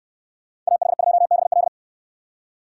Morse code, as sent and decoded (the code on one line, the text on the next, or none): .- .... ..--- -... .-..
AH2BL